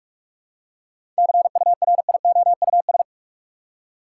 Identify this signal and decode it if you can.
Morse code, heard as a keyed tone, about 46 wpm, 700 Hz. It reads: CURIOUS
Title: Seven Letter Words - 45wpm